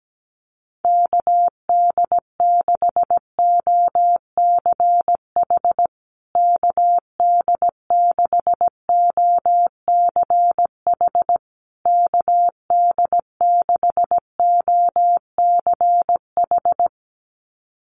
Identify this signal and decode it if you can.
Morse code, heard as a keyed tone, about 17 wpm, 700 Hz. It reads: KD6OCH KD6OCH KD6OCH
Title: USA Call Signs Repeated 3x 17wpm